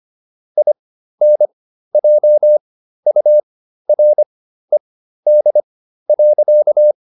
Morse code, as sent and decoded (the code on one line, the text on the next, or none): .. -. .--- ..- .-. -.. .-.-.-
INJURED.